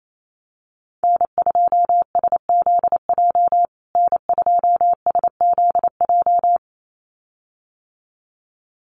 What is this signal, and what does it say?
Morse code, heard as a keyed tone, about 28 wpm, 700 Hz. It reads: N2SZJ N2SZJ
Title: USA Call Signs Repeated 2x 28wpm